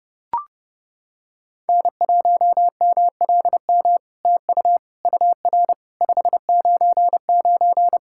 Morse code, heard as a keyed tone, about 30 wpm, 700 Hz. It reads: N1MLM TU UR 599
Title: SOTA QSO Practice 30wpm